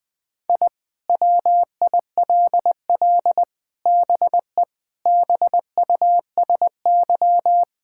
Morse code, heard as a keyed tone, about 20 wpm, 700 Hz. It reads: I WILL BE BUSY
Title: Sentences with 5-Word Phrases - 20wpm